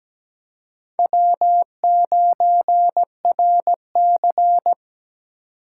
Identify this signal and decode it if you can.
Morse code, heard as a keyed tone, about 20 wpm, 700 Hz. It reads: W9RC
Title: Speed Racing - USA 1x2 Call Signs 17wpm